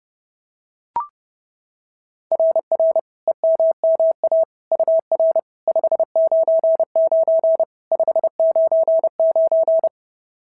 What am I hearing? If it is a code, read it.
Morse code, heard as a keyed tone, about 30 wpm, 690 Hz. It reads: RR EMMA UR 599 599
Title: SOTA QSO Practice 30wpm